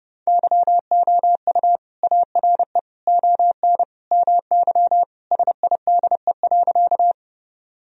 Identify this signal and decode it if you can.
Morse code, heard as a keyed tone, about 30 wpm, 700 Hz. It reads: YOU ARE ON MY SIDE.
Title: Sentences from Top 1000 Words - Review Part 1 - 30wpm